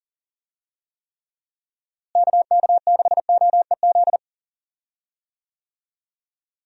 Morse code, heard as a keyed tone, about 40 wpm, 700 Hz. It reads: KK6OEZ
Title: USA Call Signs - 6 Characters - Repeated 3x - 40wpm